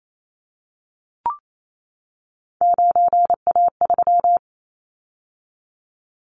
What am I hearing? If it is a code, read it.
Morse code, heard as a keyed tone, about 28 wpm, 700 Hz. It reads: 9A3
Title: Callsign Prefixes for Top 100 DXCC - 28wpm